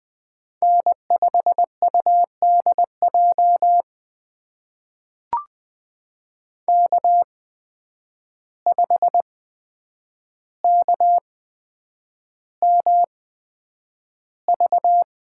Morse code, heard as a keyed tone, about 20 wpm, spaced as for 9 wpm, 700 Hz. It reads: N5UDJ K5KMV
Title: USA Call Signs - 5 Characters - 8x Character Spacing - 20wpm